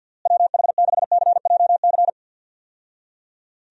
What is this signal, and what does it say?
Morse code, heard as a keyed tone, about 52 wpm, 700 Hz. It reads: WH6CJC
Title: USA Call Signs - 6 Characters - Repeated 3x - 50wpm